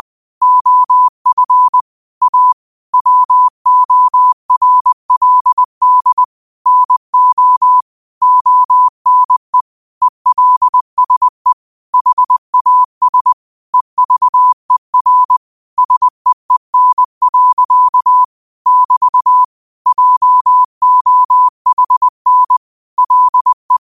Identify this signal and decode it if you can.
Morse code, heard as a keyed tone, about 20 wpm, 1000 Hz. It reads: OF A WORLD NO ONE ELSE HAS EVER SEEN. <BT> JOHN LE
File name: qotdfiles / 20wpm